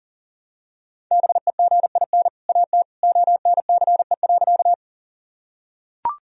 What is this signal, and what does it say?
Morse code, heard as a keyed tone, about 40 wpm, 700 Hz. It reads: BEGIN AT ONCE.
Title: Sentences from Top 1000 Words - Review Part 1 - 40wpm